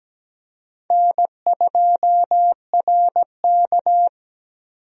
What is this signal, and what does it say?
Morse code, heard as a keyed tone, about 17 wpm, 700 Hz. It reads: N2RK
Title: USA Call Signs - 4 Characters - 17wpm